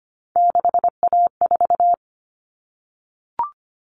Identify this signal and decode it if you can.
Morse code, heard as a keyed tone, about 25 wpm, 700 Hz. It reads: 6A4
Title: Common Cut Numbers - 3-Digit 25wpm